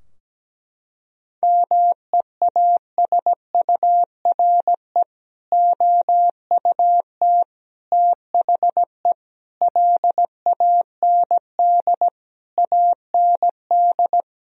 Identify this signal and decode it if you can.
Morse code, heard as a keyed tone, about 17 wpm, 700 Hz. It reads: MEASURE OUT THE LAND AND
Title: Sentences from Top 1000 Words - Review Part 2 - 17wpm